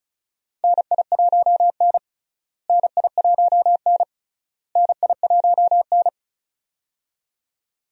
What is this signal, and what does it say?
Morse code, heard as a keyed tone, about 34 wpm, 700 Hz. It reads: NI1N NI1N NI1N
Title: USA Call Signs - 4 Characters - Repeated 3x - 35wpm